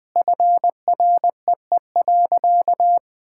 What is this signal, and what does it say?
Morse code, heard as a keyed tone, about 21 wpm, 700 Hz. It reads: FREE.
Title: Sentences with 4-Word Phrases - 20wpm